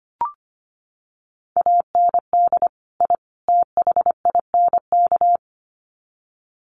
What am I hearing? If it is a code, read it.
Morse code, heard as a keyed tone, about 25 wpm, 700 Hz. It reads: AND I THINK
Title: Sets of 3 Words 25wpm